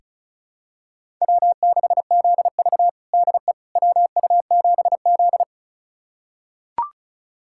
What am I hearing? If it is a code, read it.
Morse code, heard as a keyed tone, about 36 wpm, 700 Hz. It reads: W6ZV DE WU7Z
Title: USA Callsign DE Callsign - 4 Characters - 35wpm